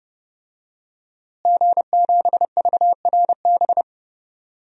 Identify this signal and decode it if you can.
Morse code, heard as a keyed tone, about 30 wpm, 700 Hz. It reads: G7VRB
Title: DXCC-100 Call Signs - 5 Characters - Repeated 3x - 30wpm